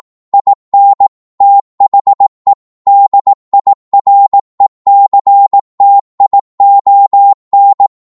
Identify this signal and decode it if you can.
Morse code, heard as a keyed tone, about 18 wpm, 800 Hz. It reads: IN THE DIRECTION